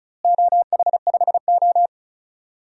Morse code, heard as a keyed tone, about 36 wpm, 700 Hz. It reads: OH5O
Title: DXCC-100 Call Signs - 4 Characters - Repeated 2x - 35wpm